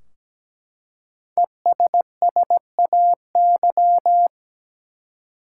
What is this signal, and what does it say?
Morse code, heard as a keyed tone, about 17 wpm, 700 Hz. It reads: ESSAY
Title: Five Letter Words - 17wpm